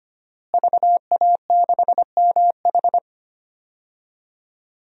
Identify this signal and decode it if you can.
Morse code, heard as a keyed tone, about 25 wpm, 700 Hz. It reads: VA6MH
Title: DXCC-100 Call Signs - 5 Characters - Repeated 3x - 25wpm